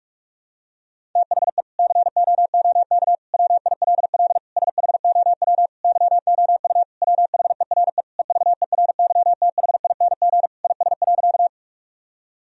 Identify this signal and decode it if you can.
Morse code, heard as a keyed tone, about 46 wpm, 700 Hz. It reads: THE COOK WILL SHOW YOU WHERE EVERYTHING IS.
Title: Sentences from Top 1000 Words - Review Part 4 - 45wpm